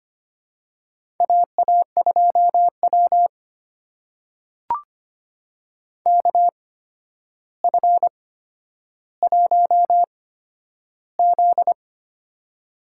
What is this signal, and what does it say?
Morse code, heard as a keyed tone, about 25 wpm, spaced as for 11 wpm, 700 Hz. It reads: AA2W KF1Z
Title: USA Call Signs - 4 Characters - 8x Character Spacing - 25wpm